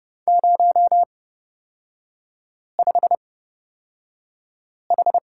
Morse code, heard as a keyed tone, about 30 wpm, spaced as for 5 wpm, 700 Hz. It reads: 05H
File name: Lesson 37e - Letter x - Three Characters 30wpm